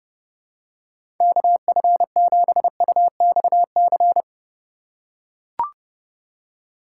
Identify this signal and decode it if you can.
Morse code, heard as a keyed tone, about 30 wpm, 700 Hz. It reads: KF7UXC
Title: K1USN SST QSO Practice - 30wpm